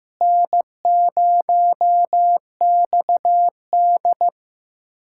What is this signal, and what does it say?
Morse code, heard as a keyed tone, about 15 wpm, 700 Hz. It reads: N0XD
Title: USA Callsign DE Callsign - 4 Characters - 15wpm